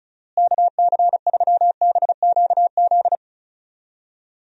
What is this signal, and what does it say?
Morse code, heard as a keyed tone, about 36 wpm, 700 Hz. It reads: KC3BQZ
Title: USA Call Signs - 6 Characters - Repeated 2x - 35wpm